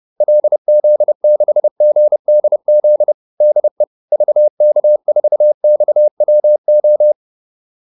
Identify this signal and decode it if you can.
Morse code, heard as a keyed tone, about 30 wpm, 600 Hz. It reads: LZ6GDZ DE VK4XWO